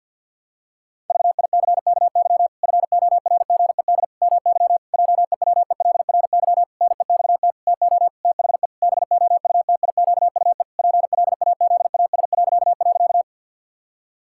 Morse code, heard as a keyed tone, about 53 wpm, 700 Hz. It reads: VICKY FORGED MY JEWELRY NEXT TO THE BOUTIQUE PLAZAS..
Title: Pangram Challenge v2 - 50wpm